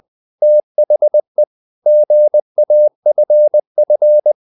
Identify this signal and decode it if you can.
Morse code, heard as a keyed tone, about 20 wpm, 600 Hz. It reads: THE GAFF
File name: the old man and the sea / the old man and the sea part 1 20wpm